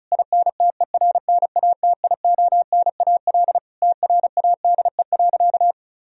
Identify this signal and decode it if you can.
Morse code, heard as a keyed tone, about 34 wpm, 700 Hz. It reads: INTERNATIONAL TRADE.
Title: Sentences from Top 1000 Words - Review Part 4 - 35wpm